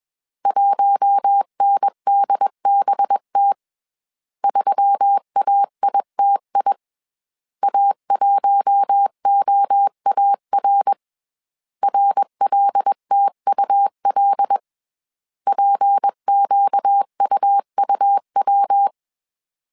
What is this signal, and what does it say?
Morse code, heard as a keyed tone, about 21 wpm, 800 Hz. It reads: 1NDBT 3AITI A1OAR RLTUL PQUUW
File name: CW / DET / 1 6 1500C S A 20wpm 2048